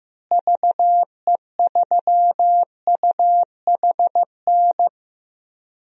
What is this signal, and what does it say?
Morse code, heard as a keyed tone, about 15 wpm, 700 Hz. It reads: VE3UHN
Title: DXCC-100 Call Signs - 6 Characters - Repeated 2x - 15wpm